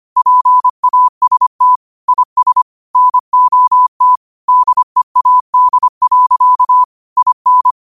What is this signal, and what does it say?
Morse code, heard as a keyed tone, about 25 wpm, 1000 Hz. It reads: PAST IS NOT DEAD. IN
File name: qotdfiles / 25wpm